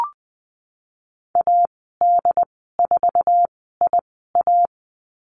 Morse code, heard as a keyed tone, about 20 wpm, spaced as for 15 wpm, 700 Hz. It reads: AD4IA
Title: 076 Lesson 13g - Letter L - Call Signs 20wpm